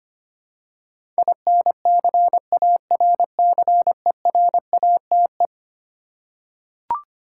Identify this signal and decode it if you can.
Morse code, heard as a keyed tone, about 25 wpm, 700 Hz. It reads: INCARCERATE
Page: morse-code-ninja-practice-25wpm